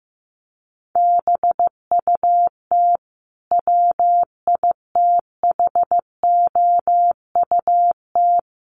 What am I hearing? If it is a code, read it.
Morse code, heard as a keyed tone, about 15 wpm, 700 Hz. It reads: BUT WITHOUT
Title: Sentences from Top 900 Words 15wpm